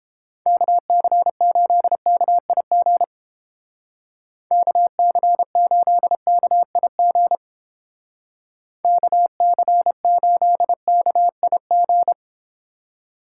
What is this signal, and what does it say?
Morse code, heard as a keyed tone, about 30 wpm, 700 Hz. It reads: KC8KIG KC8KIG KC8KIG